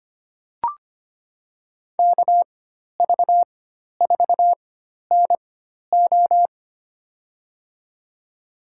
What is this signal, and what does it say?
Morse code, heard as a keyed tone, about 25 wpm, spaced as for 12 wpm, 700 Hz. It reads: KV4NO
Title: USA Call Signs - 5 Characters - 4x Character Spacing - 25wpm